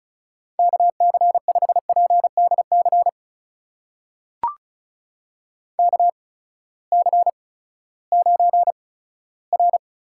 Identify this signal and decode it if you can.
Morse code, heard as a keyed tone, about 36 wpm, spaced as for 18 wpm, 700 Hz. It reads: KC5PDC KC9R
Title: USA Call Signs - 6 Characters - 8x Character Spacing - 35wpm